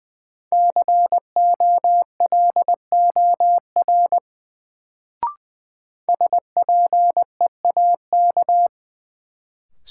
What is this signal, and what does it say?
Morse code, heard as a keyed tone, about 20 wpm, 700 Hz. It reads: COLOR SPEAK
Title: Sentences from Top 1000 Words - Review Part 2 - New Word Drill - 20wpm